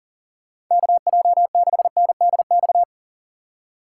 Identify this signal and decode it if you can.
Morse code, heard as a keyed tone, about 40 wpm, 700 Hz. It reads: KJ6NDX
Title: USA Call Signs - 6 Characters - 4x Character Spacing - 40wpm